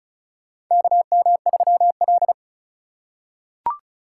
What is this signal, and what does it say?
Morse code, heard as a keyed tone, about 34 wpm, 700 Hz. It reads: KM3L
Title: USA Call Signs - 4 Characters - 8x Character Spacing - 35wpm